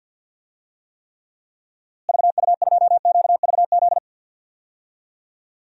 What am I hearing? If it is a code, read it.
Morse code, heard as a keyed tone, about 52 wpm, 700 Hz. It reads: VU2QVZ
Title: DXCC-100 Call Signs - 6 Characters - Repeated 3x - 50wpm